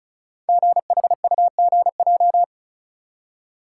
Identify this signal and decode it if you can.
Morse code, heard as a keyed tone, about 36 wpm, 690 Hz. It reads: GHUGJ
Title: Code Groups - 5 Letters - 35wpm